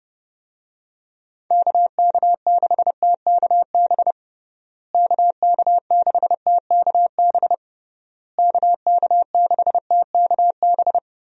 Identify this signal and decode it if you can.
Morse code, heard as a keyed tone, about 30 wpm, 700 Hz. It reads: KK6TKB KK6TKB KK6TKB
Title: USA Call Signs - 6 Characters - Repeated 3x - 30wpm